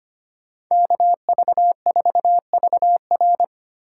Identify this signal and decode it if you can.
Morse code, heard as a keyed tone, about 25 wpm, 700 Hz. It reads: KV4VR
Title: USA Call Signs - 5 Characters - 6x Character Spacing - 25wpm